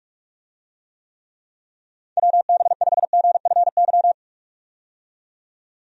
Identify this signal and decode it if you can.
Morse code, heard as a keyed tone, about 46 wpm, 700 Hz. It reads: WB5GFY
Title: USA Call Signs - 6 Characters - 45wpm